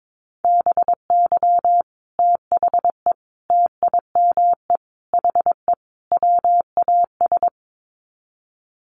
Morse code, heard as a keyed tone, about 22 wpm, 700 Hz. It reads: BY THE TIME HE WAS
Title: Sets of 5 Words Encore 22wpm